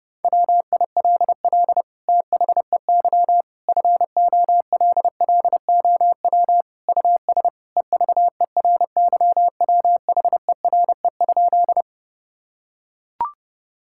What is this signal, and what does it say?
Morse code, heard as a keyed tone, about 30 wpm, 700 Hz. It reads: WILL THEY FOLLOW US EVERYWHERE?
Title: Sentences from Top 1600 Words 30wpm